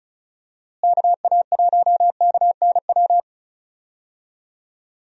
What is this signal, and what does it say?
Morse code, heard as a keyed tone, about 36 wpm, 700 Hz. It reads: KA1KNW
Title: USA Call Signs - 6 Characters - Repeated 2x - 35wpm